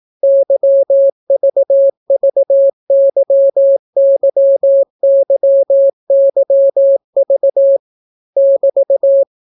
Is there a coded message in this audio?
Yes: Morse code from 0.2 to 9.2 s, heard as a keyed tone, about 18 wpm, 550 Hz. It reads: YVVYYYYV <BT>